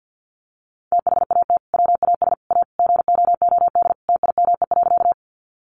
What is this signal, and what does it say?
Morse code, heard as a keyed tone, about 50 wpm, 700 Hz. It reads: THAT WAS A GOOD TIME.
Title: Sentences from Top 200 Words 50wpm